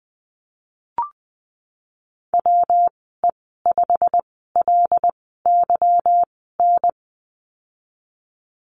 Morse code, heard as a keyed tone, about 19 wpm, spaced as for 14 wpm, 700 Hz. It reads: WE5LYN
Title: 193 Lesson 31g - Letter v - Call Signs 20wpm